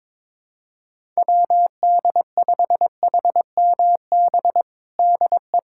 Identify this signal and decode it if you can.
Morse code, heard as a keyed tone, about 22 wpm, 700 Hz. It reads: WD5HMB DE
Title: USA Callsign DE Callsign 22wpm